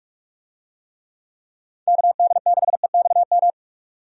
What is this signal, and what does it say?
Morse code, heard as a keyed tone, about 48 wpm, 700 Hz. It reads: KD6EXM
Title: USA Call Signs - 6 Characters - 45wpm